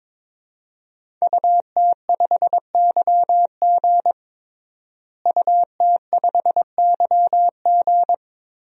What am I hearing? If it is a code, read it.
Morse code, heard as a keyed tone, about 22 wpm, 700 Hz. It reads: UT5YG UT5YG
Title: DXCC-100 Call Signs - 5 Characters - Repeated 3x - 22wpm